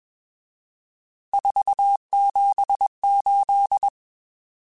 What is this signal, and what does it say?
Morse code, heard as a keyed tone, about 21 wpm, 800 Hz. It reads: 478
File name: RSTs / RST-mp3s / Various RST @20